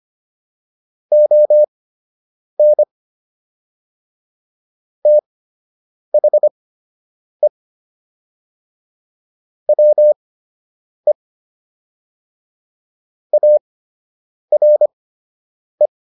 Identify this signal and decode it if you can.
Morse code, heard as a keyed tone, about 25 wpm, spaced as for 8 wpm, 600 Hz. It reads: ON THE WE ARE
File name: WD104 08